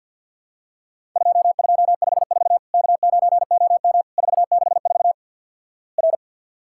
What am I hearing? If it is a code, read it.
Morse code, heard as a keyed tone, about 50 wpm, 700 Hz. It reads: 2254 K9OM 464 R
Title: Worked All Europe DX Contest - QTC Exchange with Repeats - 50wpm